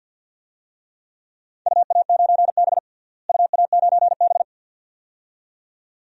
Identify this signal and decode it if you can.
Morse code, heard as a keyed tone, about 53 wpm, 700 Hz. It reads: UA9B UA9B
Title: DXCC-100 Call Signs - 4 Characters - Repeated 2x - 50wpm